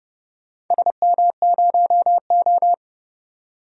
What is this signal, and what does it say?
Morse code, heard as a keyed tone, about 30 wpm, 700 Hz. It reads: SM0O
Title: DXCC-100 Call Signs - 4 Characters - 30wpm